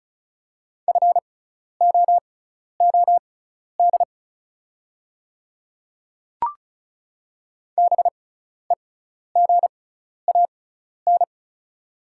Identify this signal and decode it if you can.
Morse code, heard as a keyed tone, about 36 wpm, spaced as for 12 wpm, 690 Hz. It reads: FOOD BEGAN
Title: 6x Character Spacing - Top 300 Words - 35wpm